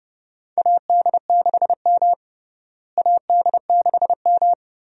Morse code, heard as a keyed tone, about 30 wpm, 700 Hz. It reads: AD6M AD6M
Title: USA Call Signs - 4 Characters - Repeated 2x - 30wpm